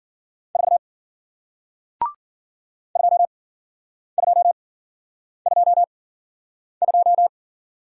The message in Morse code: 4 2222

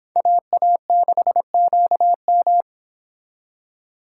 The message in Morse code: AA6QM

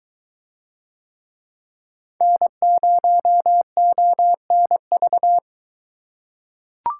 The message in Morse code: N0ONV

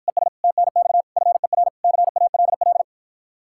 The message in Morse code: ES TNX FER CALL

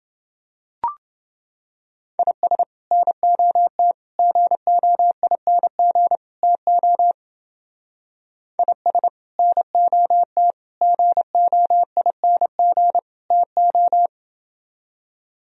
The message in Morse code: IS NOT GOING TO IS NOT GOING TO

